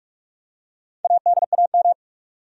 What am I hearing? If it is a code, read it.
Morse code, heard as a keyed tone, about 47 wpm, 700 Hz. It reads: ADAM
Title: USA Common Names - Max 4 Letters - 45wpm